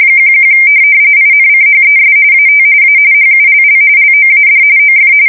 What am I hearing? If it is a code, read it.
RTTY, about 45.45 baud. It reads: 11 INCHES AT A HULL LOAD OF
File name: fsk170 / USS-New-Jersey